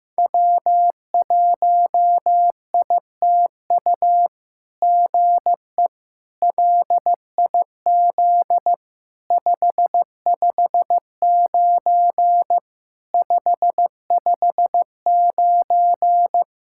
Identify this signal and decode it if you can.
Morse code, heard as a keyed tone, about 15 wpm, 700 Hz. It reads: W1ITU GE LIZ 559 559
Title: SOTA QSO Practice 15wpm